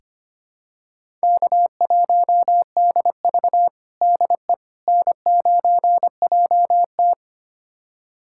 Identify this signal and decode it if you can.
Morse code, heard as a keyed tone, about 25 wpm, 700 Hz. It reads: K1DV DE N9JT
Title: USA Callsign DE Callsign - 4 Characters - 25wpm